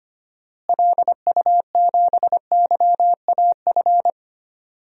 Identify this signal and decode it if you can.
Morse code, heard as a keyed tone, about 25 wpm, 700 Hz. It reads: LU7YAF
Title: DXCC-100 Call Signs - 6 Characters - Repeated 3x - 25wpm